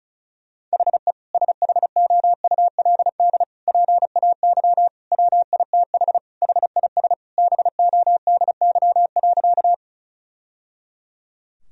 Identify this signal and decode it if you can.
Morse code, heard as a keyed tone, about 34 wpm, 700 Hz. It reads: HE SHOULD PAY WITH HIS BODY.
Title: Sentences from Top 1000 Words - Review Part 2 - 35wpm